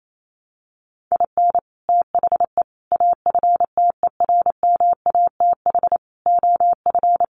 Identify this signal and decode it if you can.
Morse code, heard as a keyed tone, about 28 wpm, 700 Hz. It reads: IN THE AFTERMATH OF
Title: Sets of 5 Words 28wpm